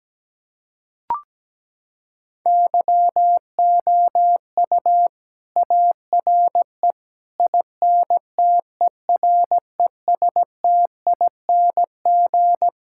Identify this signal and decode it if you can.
Morse code, heard as a keyed tone, about 17 wpm, 700 Hz. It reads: YOU ARE INTERESTING
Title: Sentences from Top 1000 Words - Review Part 6 - 17wpm